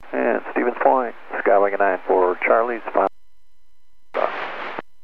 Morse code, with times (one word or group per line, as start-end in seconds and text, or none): none